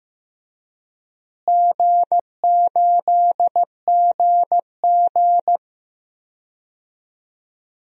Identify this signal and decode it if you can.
Morse code, heard as a keyed tone, about 15 wpm, 700 Hz. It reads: G8GG
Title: DXCC-100 Call Signs - 4 Characters - Repeated 2x - 15wpm